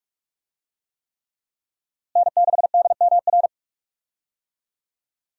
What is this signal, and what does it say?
Morse code, heard as a keyed tone, about 46 wpm, 700 Hz. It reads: N6DMR
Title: USA Call Signs - 5 Characters - Repeated 3x - 45wpm